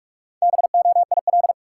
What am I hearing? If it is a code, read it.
Morse code, heard as a keyed tone, about 45 wpm, 700 Hz. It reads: BOIL
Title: Sets of 5 Words Encore - New Word Drill - 45wpm